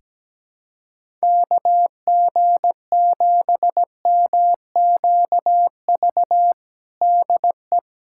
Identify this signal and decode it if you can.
Morse code, heard as a keyed tone, about 17 wpm, 700 Hz. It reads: KG7MQV DE
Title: USA Callsign DE Callsign 17wpm